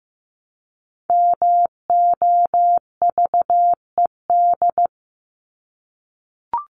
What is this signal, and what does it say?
Morse code, heard as a keyed tone, about 15 wpm, 700 Hz. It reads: MOVED
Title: ED Suffix - Top 100 Words with Base Words 15wpm